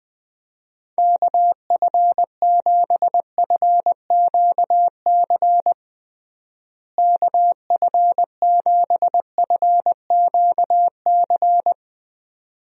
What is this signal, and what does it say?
Morse code, heard as a keyed tone, about 20 wpm, 700 Hz. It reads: KF7FQC KF7FQC
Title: USA Call Signs - 6 Characters - Repeated 2x - 20wpm